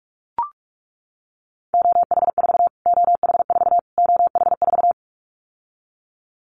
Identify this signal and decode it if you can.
Morse code, heard as a keyed tone, about 44 wpm, 700 Hz. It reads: OH4 OH4 OH4
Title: Callsign Prefixes for Top 25 DXCC - Repeated 3x - 45wpm